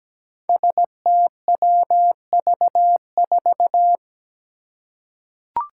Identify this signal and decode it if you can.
Morse code, heard as a keyed tone, about 17 wpm, 700 Hz. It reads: STWV4
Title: Five Challenging Letter-Number Combinations 17wpm